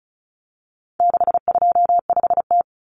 Morse code, heard as a keyed tone, about 34 wpm, 700 Hz. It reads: 625T
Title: Common Cut Numbers - 4-Digit 35wpm